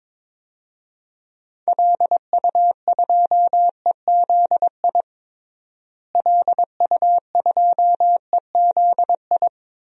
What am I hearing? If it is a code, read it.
Morse code, heard as a keyed tone, about 22 wpm, 700 Hz. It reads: LU2EZI LU2EZI
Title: DXCC-100 Call Signs - 6 Characters - Repeated 2x - 22wpm